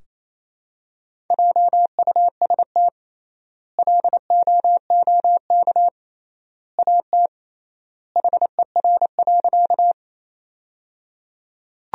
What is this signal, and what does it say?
Morse code, heard as a keyed tone, about 28 wpm, 700 Hz. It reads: JUST LOOK AT HER.
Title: 3x Word-Spacing - Sentences from Top 200 Words - 28wpm